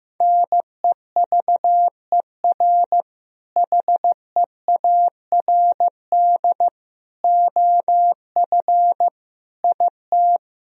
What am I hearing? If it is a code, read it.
Morse code, heard as a keyed tone, about 16 wpm, 700 Hz. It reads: NEVER HEARD OF IT
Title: Sentences from Top 400 Words and Abbreviations 15wpm